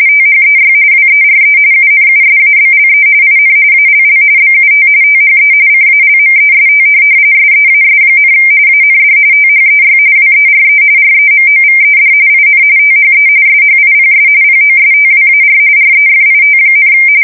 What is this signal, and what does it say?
RTTY, about 45.45 baud. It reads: THEY WERE UNUSABLE IN ALL BUT THE CALMEST SEAS, AND THEY WERE REMOVED IN THE SHIPS POSTWAR REFIT,
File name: fsk170 / USS-Massachusetts